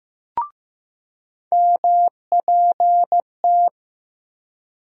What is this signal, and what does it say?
Morse code, heard as a keyed tone, about 15 wpm, 700 Hz. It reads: MPT
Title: USA Call Sign Postfixes - 3 Letters 15wpm